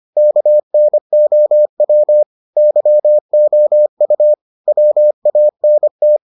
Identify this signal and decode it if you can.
Morse code, heard as a keyed tone, about 25 wpm, 600 Hz. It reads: KNOW YOU WANT